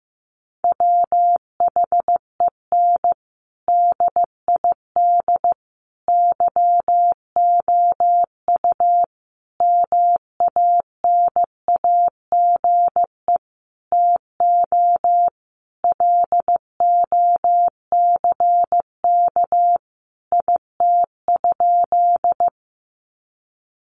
Morse code, heard as a keyed tone, about 15 wpm, 700 Hz. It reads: WHEN DID YOU MANAGE TO LOCK IT?
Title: Sentences from Top 1400 Words 15wpm